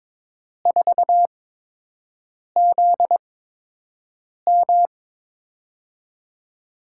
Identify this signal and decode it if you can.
Morse code, heard as a keyed tone, about 22 wpm, spaced as for 6 wpm, 700 Hz. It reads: 4ZM
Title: USA Call Signs - 8x Character Spacing - 22wpm